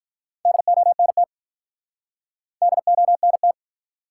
Morse code, heard as a keyed tone, about 47 wpm, 700 Hz. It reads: DONT DONT